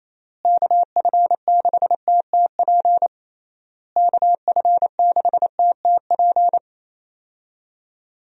Morse code, heard as a keyed tone, about 28 wpm, 700 Hz. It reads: KF6TTP KF6TTP
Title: USA Call Signs - 6 Characters - Repeated 3x - 28wpm